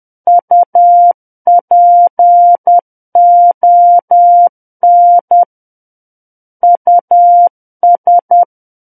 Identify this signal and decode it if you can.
Morse code, heard as a keyed tone, about 10 wpm, 700 Hz. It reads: UPON US
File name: Book One / 01-Chapter 1 - 10 WPM